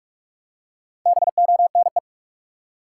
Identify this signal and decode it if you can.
Morse code, heard as a keyed tone, about 46 wpm, 700 Hz. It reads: BONE
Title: Four Letter Words - 45wpm